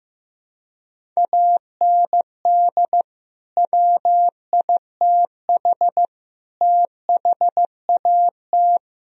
Morse code, heard as a keyed tone, about 16 wpm, 700 Hz. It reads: AND WITH THAT
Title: Sentences from Top 1000 Words - Review Part 1 - 15wpm